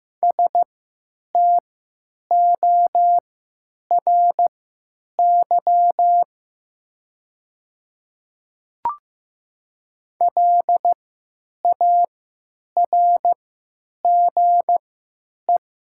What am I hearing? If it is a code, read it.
Morse code, heard as a keyed tone, about 15 wpm, spaced as for 9 wpm, 700 Hz. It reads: STORY LARGE
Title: 3x Character Spacing - Top 500 Words - 15wpm